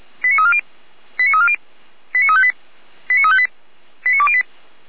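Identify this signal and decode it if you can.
Five-tone selective call (ZVEI), heard as quick runs of five tones, tones of about 70 ms. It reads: 89340 89340 89358 89358 89208
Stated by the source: ZVEI five-tone selective call received with an SDR.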